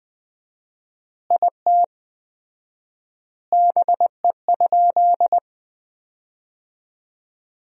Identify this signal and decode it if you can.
Morse code, heard as a keyed tone, about 20 wpm, 700 Hz. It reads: IT BE?
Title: 4x Word-Spacing - Sentences from Top 100 Words - 20wpm